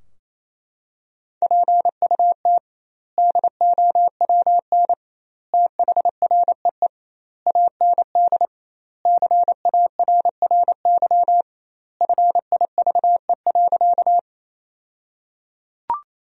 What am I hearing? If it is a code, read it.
Morse code, heard as a keyed tone, about 28 wpm, 700 Hz. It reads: PUT DOWN THREE AND CARRY FIVE.
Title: 2x Word-Spacing - Sentences from Top 400 Words - 28wpm